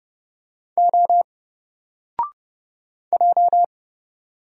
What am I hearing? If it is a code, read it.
Morse code, heard as a keyed tone, about 30 wpm, 700 Hz. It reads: O J
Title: Single Character (Letter:Number:Punctuation:Prosign) 30wpm